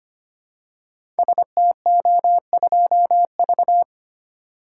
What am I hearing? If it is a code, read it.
Morse code, heard as a keyed tone, about 26 wpm, 700 Hz. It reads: STO2V
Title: Code Groups - 5 Letter:Numbers - 25wpm